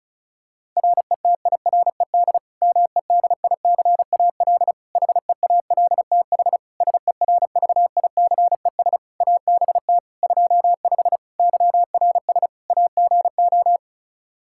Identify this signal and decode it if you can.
Morse code, heard as a keyed tone, about 34 wpm, 700 Hz. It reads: RETIRED MEDICAL HEALTH SERVICES ABT 25 YRS AGO
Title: Practice QSO - Exchange 3 - More Info - 35wpm